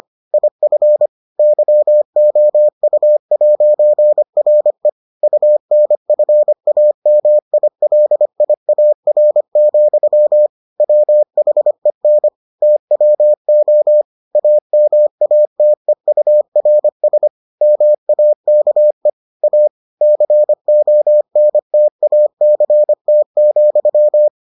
Morse code, heard as a keyed tone, about 25 wpm, 600 Hz. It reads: IF YOU'RE UNFAMILIAR, WHEN TWO AMATEURS MAKE A CONTACT,